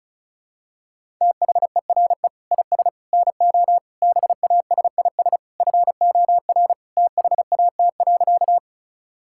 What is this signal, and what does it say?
Morse code, heard as a keyed tone, about 36 wpm, 700 Hz. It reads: THERE IS NO BASIS FOR THAT.
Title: Sentences from Top 1000 Words - Review Part 5 - 35wpm